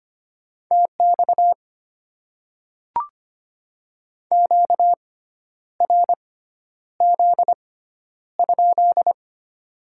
Morse code, heard as a keyed tone, about 25 wpm, spaced as for 10 wpm, 700 Hz. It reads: TX QRZ?
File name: Top 90 QSO Elements - 6x Character Spacing - 25wpm